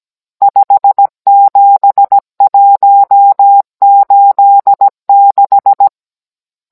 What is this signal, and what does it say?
Morse code, heard as a keyed tone, about 17 wpm, 800 Hz. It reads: 57186
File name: Cours-NR18-16mm